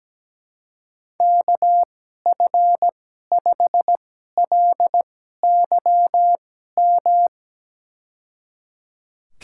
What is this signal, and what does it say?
Morse code, heard as a keyed tone, about 17 wpm, spaced as for 12 wpm, 700 Hz. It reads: KF5LYM
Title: USA Call Signs - 6 Characters - 2x Character Spacing - 17wpm